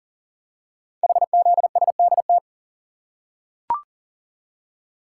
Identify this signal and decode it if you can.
Morse code, heard as a keyed tone, about 38 wpm, 700 Hz. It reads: HZSDT
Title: Code Groups - 5 Letters - 40wpm